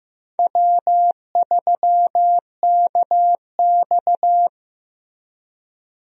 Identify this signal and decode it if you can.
Morse code, heard as a keyed tone, about 15 wpm, 700 Hz. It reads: W3KX
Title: USA Call Signs - 4 Characters - Repeated 2x - 15wpm